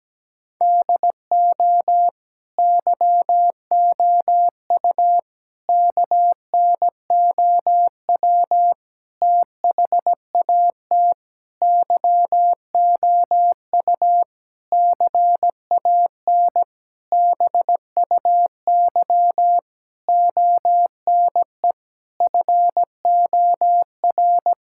Morse code, heard as a keyed tone, about 17 wpm, 700 Hz. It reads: DO YOU KNOW THAT YOU CAN BUY ONE FOR
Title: Sentences from Top 1000 Words - Review Part 5 - 17wpm